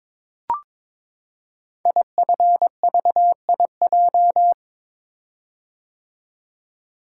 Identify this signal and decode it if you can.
Morse code, heard as a keyed tone, about 22 wpm, 700 Hz. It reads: IFVIJ
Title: Code Groups - 5 Letters - 22wpm